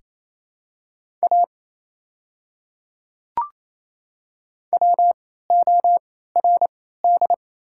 Morse code, heard as a keyed tone, about 28 wpm, spaced as for 16 wpm, 700 Hz. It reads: A WORD
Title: 3x Character Spacing - Top 100 Words - 28wpm